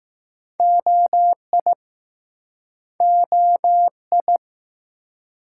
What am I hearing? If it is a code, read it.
Morse code, heard as a keyed tone, about 20 wpm, 700 Hz. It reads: OI OI